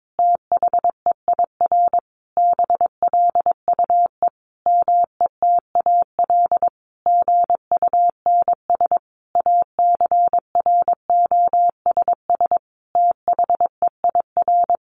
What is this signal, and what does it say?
Morse code, heard as a keyed tone, about 22 wpm, 700 Hz. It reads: THEIR BLUE METAL GUNS ACROSS THEIR
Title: A Sound Of Thunder 22wpm